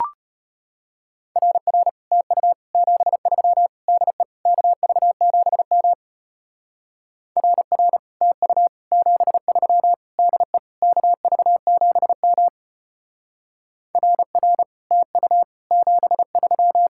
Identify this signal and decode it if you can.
Morse code, heard as a keyed tone, about 32 wpm, 700 Hz. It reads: RR TU 73 DE KV7M RR TU 73 DE KV7M RR TU 73